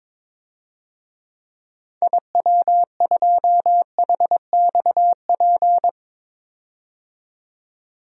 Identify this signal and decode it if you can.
Morse code, heard as a keyed tone, about 23 wpm, 700 Hz. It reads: IW2HXP
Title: DXCC-100 Call Signs - 6 Characters - 22wpm